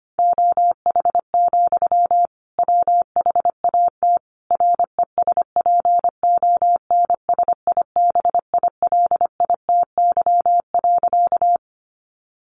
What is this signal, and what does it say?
Morse code, heard as a keyed tone, about 25 wpm, 700 Hz. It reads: OH, WHAT RESPONSIBILITY.
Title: Sentences from Top 1200 Words 25wpm